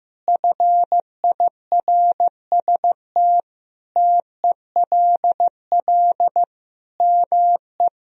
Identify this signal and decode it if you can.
Morse code, heard as a keyed tone, about 15 wpm, 700 Hz. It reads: FIRST TELL ME
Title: Sentences from Top 1000 Words - Review Part 2 - 15wpm